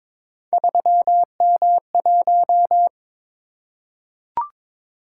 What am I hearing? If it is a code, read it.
Morse code, heard as a keyed tone, about 22 wpm, 700 Hz. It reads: 3M1
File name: practice / 22wpm / Three Letter-Number Combinations - 22wpm